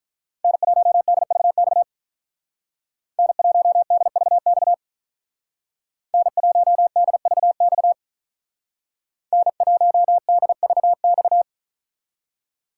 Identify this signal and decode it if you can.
Morse code, heard as a keyed tone, about 43 wpm, 700 Hz. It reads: N1DVX N1DVX N1DVX N1DVX
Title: Speed Racing - USA 1x3 Call Signs 35wpm